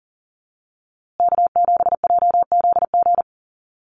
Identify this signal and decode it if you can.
Morse code, heard as a keyed tone, about 40 wpm, 700 Hz. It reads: K7JZG
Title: USA Call Signs Repeated 3x 40wpm